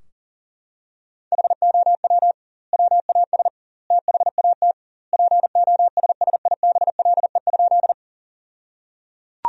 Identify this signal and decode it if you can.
Morse code, heard as a keyed tone, about 37 wpm, 700 Hz. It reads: HOW WAS THAT POSSIBLE?
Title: 2x Word-Spacing - Sentences from Top 500 Words - 40wpm